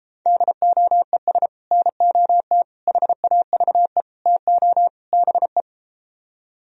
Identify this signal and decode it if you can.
Morse code, heard as a keyed tone, about 33 wpm, 700 Hz. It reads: DOES NOT HAVE TO BE